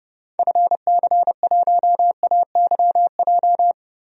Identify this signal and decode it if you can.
Morse code, heard as a keyed tone, about 32 wpm, 700 Hz. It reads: FC1AYJ
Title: DXCC-100 Call Signs - 6 Characters - Repeated 3x - 30wpm